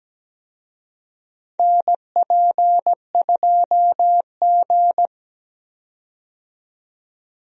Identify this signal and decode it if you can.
Morse code, heard as a keyed tone, about 17 wpm, 700 Hz. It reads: NP2G
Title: USA Call Signs - 4 Characters - 17wpm